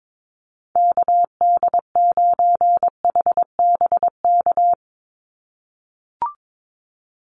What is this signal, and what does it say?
Morse code, heard as a keyed tone, about 22 wpm, 700 Hz. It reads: KD9HBK
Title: USA Call Signs 22wpm